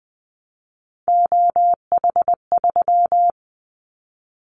Morse code, heard as a keyed tone, about 20 wpm, 700 Hz. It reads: OH3
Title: Callsign Prefixes for Top 50 DXCC - 20wpm